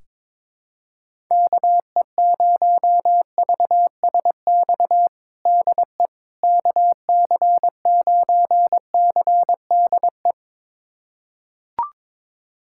Morse code, heard as a keyed tone, about 22 wpm, 700 Hz. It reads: KE0VSX DE KC9CDE
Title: USA Callsign DE Callsign 22wpm